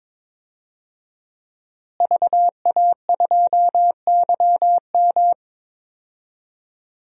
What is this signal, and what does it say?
Morse code, heard as a keyed tone, about 22 wpm, 700 Hz. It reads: VA2YM
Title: DXCC-100 Call Signs - 5 Characters - Repeated 3x - 22wpm